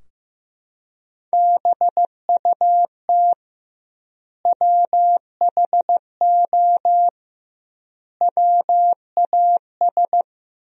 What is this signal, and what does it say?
Morse code, heard as a keyed tone, about 15 wpm, 700 Hz. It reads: BUT WHO WAS
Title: 2x Word-Spacing - Sentences from Top 100 Words - 15wpm